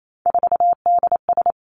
Morse code, heard as a keyed tone, about 28 wpm, 700 Hz. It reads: 4DS
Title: Three Challenging Letter-Number Combinations 28wpm